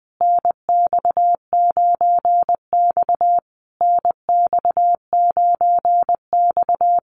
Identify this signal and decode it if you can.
Morse code, heard as a keyed tone, about 19 wpm, 700 Hz. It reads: NX9X NX9X
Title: Common QSO Phrases Encore 20wpm